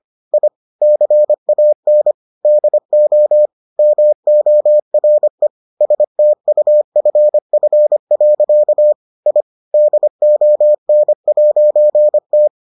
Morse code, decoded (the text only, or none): I CAN DO MORE STUFF. I DON'T